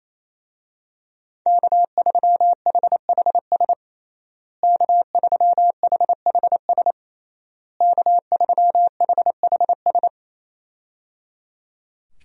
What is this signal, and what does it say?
Morse code, heard as a keyed tone, about 28 wpm, 700 Hz. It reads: K3HHS K3HHS K3HHS
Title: USA Call Signs - 5 Characters - Repeated 3x - 28wpm